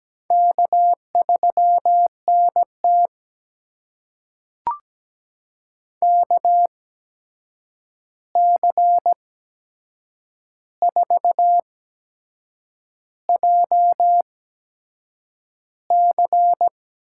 Morse code, heard as a keyed tone, about 17 wpm, spaced as for 7 wpm, 700 Hz. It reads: K3NT KC4JC
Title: USA Call Signs - 8x Character Spacing - 17wpm